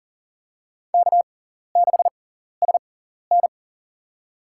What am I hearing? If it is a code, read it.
Morse code, heard as a keyed tone, about 40 wpm, spaced as for 14 wpm, 700 Hz. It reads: K6SN
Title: USA Call Signs - 4 Characters - 6x Character Spacing - 40wpm